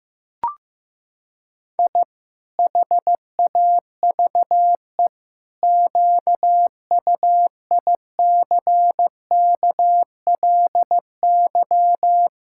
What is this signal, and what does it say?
Morse code, heard as a keyed tone, about 15 wpm, 700 Hz. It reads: I HAVE QUICKLY
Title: Pangram Challenge v2 - 15wpm